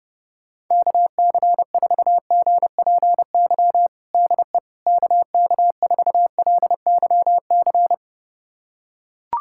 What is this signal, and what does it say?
Morse code, heard as a keyed tone, about 32 wpm, 700 Hz. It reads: KC4GPY DE KK4LYC
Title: USA Callsign DE Callsign - 6 Characters - 30wpm